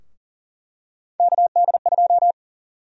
Morse code, heard as a keyed tone, about 40 wpm, 700 Hz. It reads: KD2
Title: USA Call Sign Prefixes - Repeated 2x - 40wpm